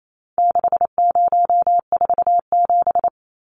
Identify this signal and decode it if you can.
Morse code, heard as a keyed tone, about 28 wpm, 700 Hz. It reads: 6047
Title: Numbers - 4-Digit 28wpm